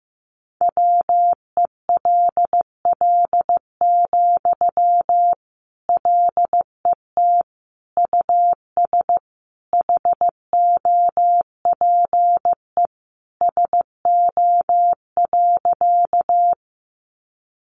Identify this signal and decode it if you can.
Morse code, heard as a keyed tone, about 15 wpm, 700 Hz. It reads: WELL, LET US HOPE SO.
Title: Sentences from Top 1000 Words 15wpm